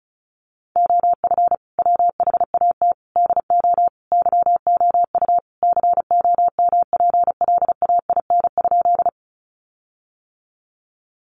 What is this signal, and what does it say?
Morse code, heard as a keyed tone, about 34 wpm, 700 Hz. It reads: OF WHAT DO YOU COMPLAIN?
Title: Sentences from Top 1300 Words 35wpm